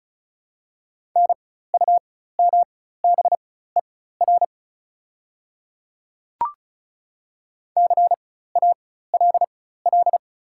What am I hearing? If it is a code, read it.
Morse code, heard as a keyed tone, about 36 wpm, spaced as for 16 wpm, 690 Hz. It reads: NUMBER CALL
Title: 4x Character Spacing - Top 100 Words - 35wpm